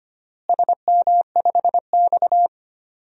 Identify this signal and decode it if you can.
Morse code, heard as a keyed tone, about 25 wpm, 700 Hz. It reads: SM5X
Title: DXCC-100 Call Signs - 4 Characters - Repeated 2x - 25wpm